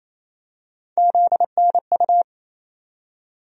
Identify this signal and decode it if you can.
Morse code, heard as a keyed tone, about 28 wpm, 700 Hz. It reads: ZNU
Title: USA Call Sign Postfixes - 3 Letters 28wpm